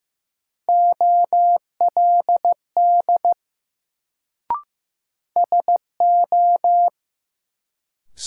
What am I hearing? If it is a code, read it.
Morse code, heard as a keyed tone, about 15 wpm, 700 Hz. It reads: OLD SO
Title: Sets of 2 Words - New Word Drill - 15wpm